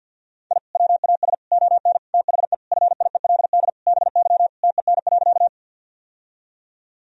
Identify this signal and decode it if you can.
Morse code, heard as a keyed tone, about 47 wpm, 700 Hz. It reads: I WAS ON THE FIELD BY TEN.
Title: Sentences from Top 1000 Words - Review Part 2 - 50wpm